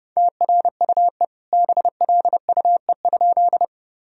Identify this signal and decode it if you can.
Morse code, heard as a keyed tone, about 30 wpm, 700 Hz. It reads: TRUE BLUE?
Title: Sentences from Top 1000 Words - Review Part 3 - 30wpm